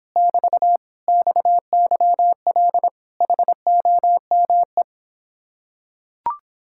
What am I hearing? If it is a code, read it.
Morse code, heard as a keyed tone, about 26 wpm, 700 Hz. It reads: <BT> XYL HOME